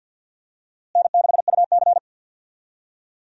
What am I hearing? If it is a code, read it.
Morse code, heard as a keyed tone, about 53 wpm, 700 Hz. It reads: N6UC
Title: USA Call Signs - 4 Characters - Repeated 3x - 50wpm